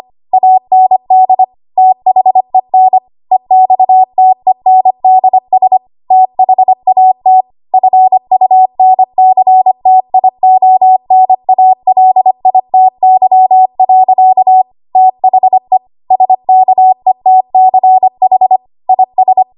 Morse code, heard as a keyed tone, about 25 wpm, 750 Hz. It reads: AND THEN EXTENDS THAT FUNCTIONALITY. THE SKETCH IS
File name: Archive / 25 WPM / 25WPM